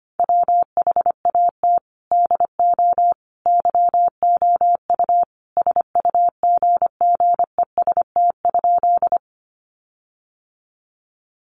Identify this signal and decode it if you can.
Morse code, heard as a keyed tone, about 25 wpm, 700 Hz. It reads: WHAT DO YOU SUGGEST?
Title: Sentences from Top 1300 Words 25wpm